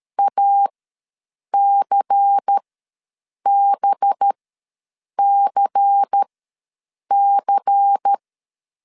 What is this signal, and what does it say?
Morse code, heard as a keyed tone, about 13 wpm, spaced as for 7 wpm, 800 Hz. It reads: ACBCC
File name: LEZIONI / 5 LBVC / 5 LBVC ws 12wpm 3159